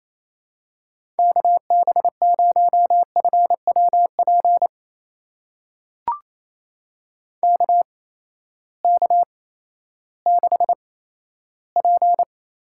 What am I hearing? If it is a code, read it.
Morse code, heard as a keyed tone, about 28 wpm, spaced as for 14 wpm, 700 Hz. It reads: KB0FWP KK6P
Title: USA Call Signs - 6 Characters - 8x Character Spacing - 28wpm